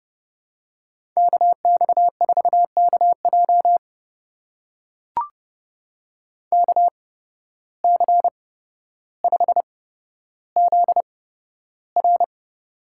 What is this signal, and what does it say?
Morse code, heard as a keyed tone, about 30 wpm, spaced as for 13 wpm, 700 Hz. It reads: KX4KJ KC5ZR
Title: USA Call Signs - 5 Characters - 8x Character Spacing - 30wpm